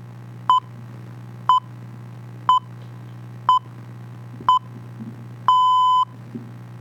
Radio time signal (time pips).